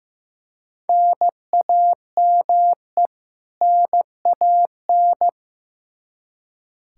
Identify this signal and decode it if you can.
Morse code, heard as a keyed tone, about 15 wpm, 700 Hz. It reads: NAME NAN
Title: Practice QSO - Exchange 1 - Beginning - 15wpm